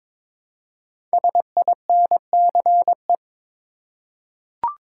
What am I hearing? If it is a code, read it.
Morse code, heard as a keyed tone, about 22 wpm, 700 Hz. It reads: SINCE
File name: 5 Letter Words - First 3 Letters Given - 22wpm